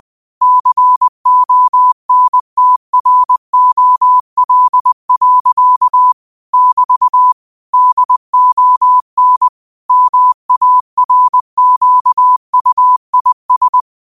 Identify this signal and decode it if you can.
Morse code, heard as a keyed tone, about 20 wpm, 1000 Hz. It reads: CONTROL. <BT> DON MARQUIS
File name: qotdfiles / 20wpm